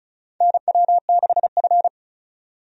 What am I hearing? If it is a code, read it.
Morse code, heard as a keyed tone, about 34 wpm, 700 Hz. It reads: NW6F
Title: USA Call Signs - 4 Characters - 35wpm